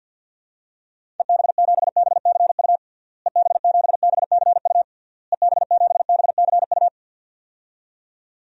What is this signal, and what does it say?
Morse code, heard as a keyed tone, about 52 wpm, 700 Hz. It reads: EB7BCU EB7BCU EB7BCU
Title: DXCC-100 Call Signs - 6 Characters - Repeated 3x - 50wpm